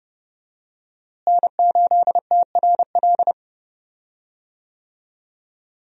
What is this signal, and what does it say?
Morse code, heard as a keyed tone, about 30 wpm, 700 Hz. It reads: N8TRL
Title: USA Call Signs - 5 Characters - Repeated 2x - 30wpm